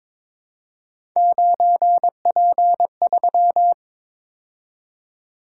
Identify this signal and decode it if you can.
Morse code, heard as a keyed tone, about 22 wpm, 700 Hz. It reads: 9P3
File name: practice / 22wpm / Three Letter-Number Combinations - 22wpm